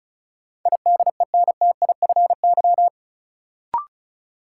Morse code, heard as a keyed tone, about 36 wpm, 700 Hz. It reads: IDENTIFY
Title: Sentences from Top 900 Words - New Word Drill - 35wpm